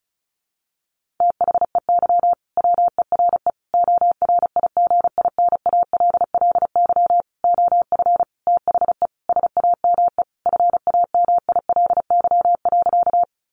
Morse code, heard as a keyed tone, about 34 wpm, 700 Hz. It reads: THEY WERE ORIGINALLY OF THE SAME FAMILY.
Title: Sentences from Top 1500 Words 35wpm